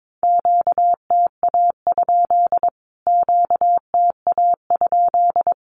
Morse code, heard as a keyed tone, about 22 wpm, 700 Hz. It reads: QTA? QTA?
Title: Q-Codes Repeated 3x 22wpm